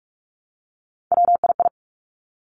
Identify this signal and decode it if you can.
Morse code, heard as a keyed tone, about 46 wpm, 700 Hz. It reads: WII
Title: Three Challenging Letter-Number Combinations 45wpm